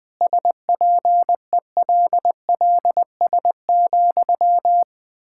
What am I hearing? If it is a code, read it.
Morse code, heard as a keyed tone, about 20 wpm, 700 Hz. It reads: SPELLS,
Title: Pangram Challenge v2 - 20wpm